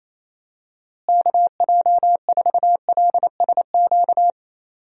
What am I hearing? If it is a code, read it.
Morse code, heard as a keyed tone, about 28 wpm, 700 Hz. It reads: KJ4LSQ
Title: USA Call Signs - 6 Characters - Repeated 2x - 28wpm